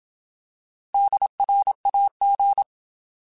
Morse code, heard as a keyed tone, about 27 wpm, 800 Hz. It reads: DRAG
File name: Alphabetical / Alphabetical mp3s / Ds @25